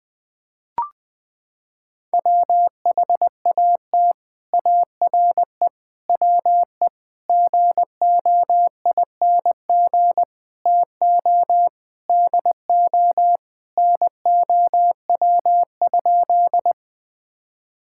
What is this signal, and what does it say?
Morse code, heard as a keyed tone, about 19 wpm, 700 Hz. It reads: WHAT ARE WE GOING TO DO NOW?
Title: Sentences with 6-Word Phrases - 20wpm